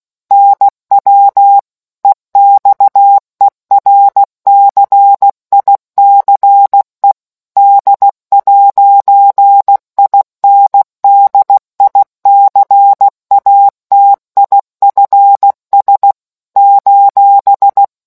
Morse code, heard as a keyed tone, about 16 wpm, 800 Hz. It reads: NW EXERCICE D'INDICATIFS :